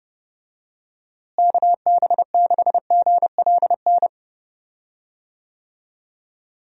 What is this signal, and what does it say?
Morse code, heard as a keyed tone, about 30 wpm, 700 Hz. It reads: KB6GLN
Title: USA Call Signs - 6 Characters - Repeated 3x - 30wpm